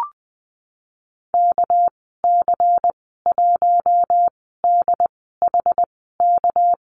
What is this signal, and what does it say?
Morse code, heard as a keyed tone, about 20 wpm, spaced as for 14 wpm, 700 Hz. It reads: KC1DHK
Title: 231 Lesson 37g - Letter x - Call Signs 20wpm